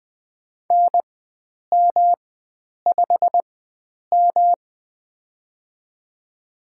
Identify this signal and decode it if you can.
Morse code, heard as a keyed tone, about 21 wpm, spaced as for 9 wpm, 700 Hz. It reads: NM5M
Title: USA Call Signs - 4 Characters - 4x Character Spacing - 20wpm